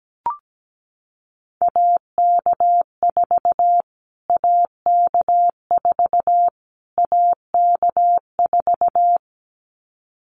Morse code, heard as a keyed tone, about 17 wpm, 700 Hz. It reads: AK4 AK4 AK4
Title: Callsign Prefixes for Top 25 DXCC - Repeated 3x - 17wpm